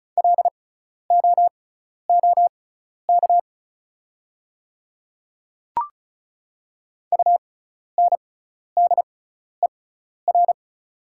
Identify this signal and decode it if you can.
Morse code, heard as a keyed tone, about 34 wpm, spaced as for 12 wpm, 700 Hz. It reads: LOOK UNDER
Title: 6x Character Spacing - Top 200 Words - 35wpm